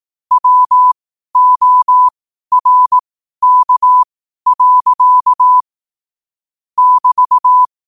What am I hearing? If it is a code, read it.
Morse code, heard as a keyed tone, about 18 wpm, spaced as for 12 wpm, 1000 Hz. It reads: WORK. <BT>
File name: qotdfiles / 12wpm